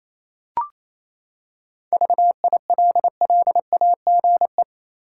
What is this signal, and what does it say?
Morse code, heard as a keyed tone, about 28 wpm, 700 Hz. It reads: VILLAGE
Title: Seven Letter Words - 28wpm